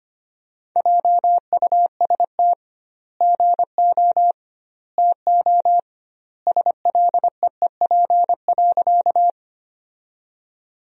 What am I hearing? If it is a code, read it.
Morse code, heard as a keyed tone, about 25 wpm, 700 Hz. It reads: JUST GO TO SLEEP.
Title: 2x Word-Spacing - Sentences from Top 500 Words - 25wpm